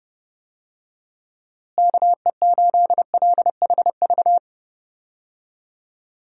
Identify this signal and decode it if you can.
Morse code, heard as a keyed tone, about 32 wpm, 700 Hz. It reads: KE8LHV
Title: USA Call Signs - 6 Characters - Repeated 3x - 30wpm